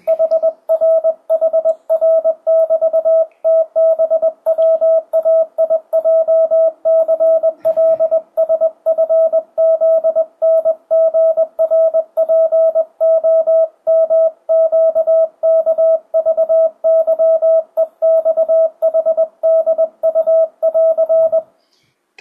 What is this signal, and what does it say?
Morse code, heard as a keyed tone, about 20 wpm, 650 Hz. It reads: HRHR<BT>TBWAIJCLSFZNGRPOMQKVYEXHDU<AR>
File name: 2025cw / speedup / 21wpm